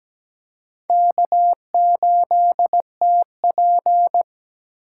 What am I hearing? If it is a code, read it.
Morse code, heard as a keyed tone, about 17 wpm, 700 Hz. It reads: K8TP
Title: USA Call Signs - 4x Character Spacing - 17wpm